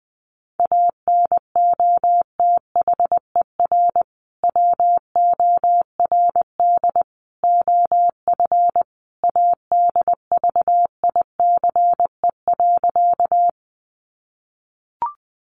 Morse code, heard as a keyed tone, about 20 wpm, 700 Hz. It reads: ANOTHER WORD OF ADVICE.
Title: Sentences from Top 1100 Words 20wpm